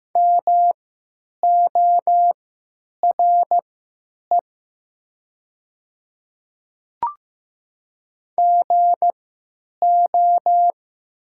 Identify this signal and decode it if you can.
Morse code, heard as a keyed tone, about 15 wpm, spaced as for 9 wpm, 690 Hz. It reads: MORE GO
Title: 3x Character Spacing - Top 100 Words - 15wpm